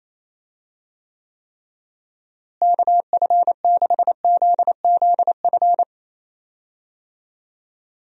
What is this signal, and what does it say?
Morse code, heard as a keyed tone, about 28 wpm, 700 Hz. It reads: KF6ZZF
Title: USA Call Signs - 6 Characters - Repeated 3x - 28wpm